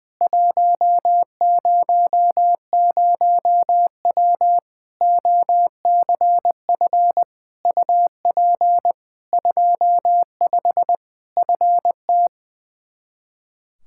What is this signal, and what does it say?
Morse code, heard as a keyed tone, about 20 wpm, 700 Hz. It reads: 100W OCF UP 25 FT
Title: Practice QSO - Exchange 2 - Info - 20wpm